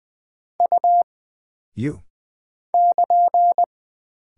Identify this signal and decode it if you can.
Morse code, heard as a keyed tone, about 20 wpm, 700 Hz. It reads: U <KN>
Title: Single Character (Letter:Number:Punctuation:Prosign) - Mind-Melt - 20wpm